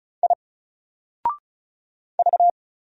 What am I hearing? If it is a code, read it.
Morse code, heard as a keyed tone, about 33 wpm, 700 Hz. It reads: I V
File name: Single-Letter-Number-35wpm